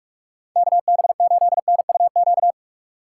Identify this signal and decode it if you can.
Morse code, heard as a keyed tone, about 46 wpm, 700 Hz. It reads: KB8NUQ
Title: USA Call Signs - 6 Characters - 2x Character Spacing - 45wpm